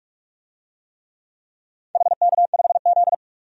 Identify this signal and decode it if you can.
Morse code, heard as a keyed tone, about 46 wpm, 700 Hz. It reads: HK5Z
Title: DXCC-100 Call Signs - 4 Characters - 45wpm